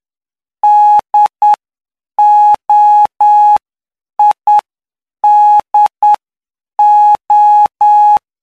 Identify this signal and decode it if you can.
Morse code, heard as a keyed tone, about 9 wpm, spaced as for 7 wpm, 840 Hz. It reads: DOIDO
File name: cw / py2lcd / 04-PY2LCD-Li��o 13